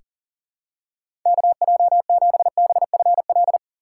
Morse code, heard as a keyed tone, about 40 wpm, 700 Hz. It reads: KJ7BFL
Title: USA Call Signs - 4x Character Spacing - 40wpm